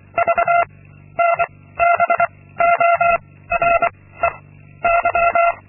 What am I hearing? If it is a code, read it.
Morse code, heard as a keyed tone, about 24 wpm, 680 Hz. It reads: V N B O R E Y
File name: Linksmakalnis-2